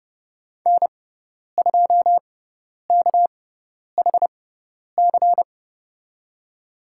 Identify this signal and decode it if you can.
Morse code, heard as a keyed tone, about 30 wpm, spaced as for 10 wpm, 700 Hz. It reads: N2KHC
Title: USA Call Signs - 6x Character Spacing - 30wpm